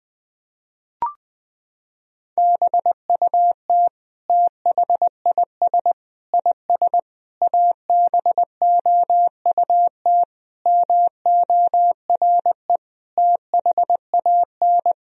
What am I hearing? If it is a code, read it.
Morse code, heard as a keyed tone, about 21 wpm, 700 Hz. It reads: BUT THIS IS ABOUT MORE THAN
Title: Sentences from Top 1000 Words - Review Part 3 - 20wpm